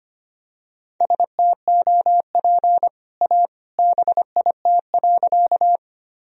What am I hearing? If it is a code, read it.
Morse code, heard as a keyed tone, about 25 wpm, 700 Hz. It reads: STOP A BIT.
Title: Sentences from Top 1000 Words - Review Part 1 - 25wpm